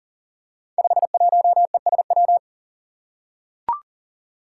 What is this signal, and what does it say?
Morse code, heard as a keyed tone, about 40 wpm, 700 Hz. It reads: 51ESW
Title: Five Challenging Letter-Number Combinations 40wpm